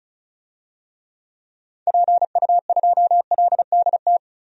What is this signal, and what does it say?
Morse code, heard as a keyed tone, about 36 wpm, 700 Hz. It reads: PU2LDT
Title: DXCC-100 Call Signs - 6 Characters - 35wpm